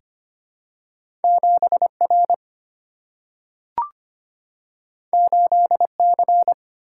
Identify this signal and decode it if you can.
Morse code, heard as a keyed tone, about 25 wpm, 700 Hz. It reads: 7R 8C
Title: Single-Letter Single-Number 25wpm